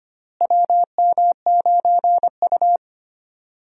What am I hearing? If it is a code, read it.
Morse code, heard as a keyed tone, about 25 wpm, 700 Hz. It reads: WM9U
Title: USA Call Signs - 4 Characters - Repeated 2x - 25wpm